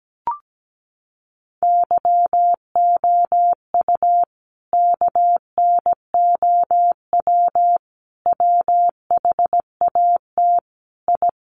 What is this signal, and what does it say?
Morse code, heard as a keyed tone, about 17 wpm, 700 Hz. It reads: YOU KNOW WHAT I
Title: Sets of 5 Words 17wpm